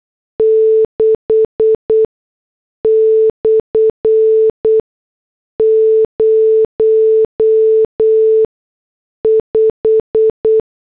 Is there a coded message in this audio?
Yes: Morse code from 0.4 to 10.6 s, heard as a keyed tone, about 8 wpm, spaced as for 6 wpm, 440 Hz. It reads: 6/05